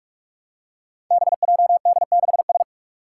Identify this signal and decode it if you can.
Morse code, heard as a keyed tone, about 45 wpm, 700 Hz. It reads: BJD6S
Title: Five Challenging Letter-Number Combinations 45wpm